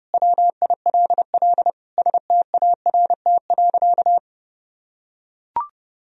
Morse code, heard as a keyed tone, about 30 wpm, 700 Hz. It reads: WILL START.
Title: Sentences from Top 300 Words and Abbreviations 30wpm